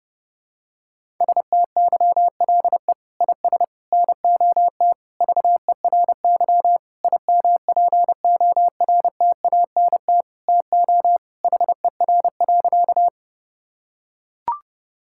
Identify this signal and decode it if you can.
Morse code, heard as a keyed tone, about 32 wpm, 700 Hz. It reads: STYLE IS NOT VERY IMPORTANT TO HER.
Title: Sentences from Top 1000 Words - Review Part 5 - 30wpm